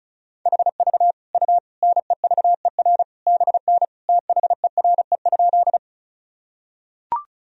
Morse code, heard as a keyed tone, about 34 wpm, 700 Hz. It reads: HV U NEVER BN THERE?
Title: Sentences from Top 300 Words and Abbreviations 35wpm